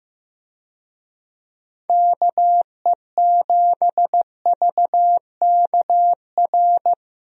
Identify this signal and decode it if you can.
Morse code, heard as a keyed tone, about 16 wpm, 700 Hz. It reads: KE7VKR
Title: USA Call Signs - 6 Characters - Repeated 3x - 15wpm